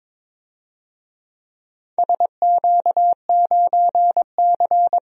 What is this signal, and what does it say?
Morse code, heard as a keyed tone, about 22 wpm, 700 Hz. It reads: SQ9C
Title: DXCC-100 Call Signs - 4 Characters - 22wpm